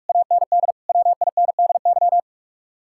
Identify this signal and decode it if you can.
Morse code, heard as a keyed tone, about 54 wpm, 700 Hz. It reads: AND WINDY
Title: Speed-Racing Common QSO Phrases Encore - 45wpm